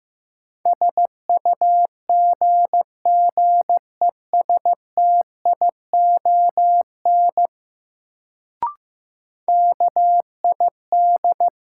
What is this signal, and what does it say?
Morse code, heard as a keyed tone, about 15 wpm, 700 Hz. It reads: SUGGESTION KID
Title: Sentences from Top 1600 Words - New Word Drill - 15wpm